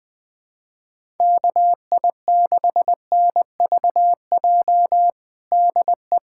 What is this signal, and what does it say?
Morse code, heard as a keyed tone, about 21 wpm, 700 Hz. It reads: KI6NVJ DE
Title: USA Callsign DE Callsign - 6 Characters - 20wpm